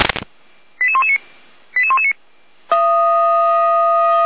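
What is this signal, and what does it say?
Five-tone selective call (ZVEI), heard as quick runs of five tones, tones of about 80 ms. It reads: 88119 88119
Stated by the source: ZVEI five-tone selective call received on an SDR.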